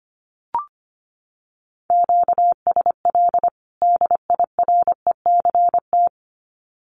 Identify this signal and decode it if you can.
Morse code, heard as a keyed tone, about 25 wpm, 700 Hz. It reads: QSL DIRECT
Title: Common QSO Phrases 25wpm